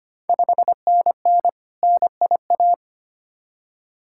Morse code, heard as a keyed tone, about 25 wpm, 700 Hz. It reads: 5NN NIA
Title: Ontario QSO Party Exchange 25wpm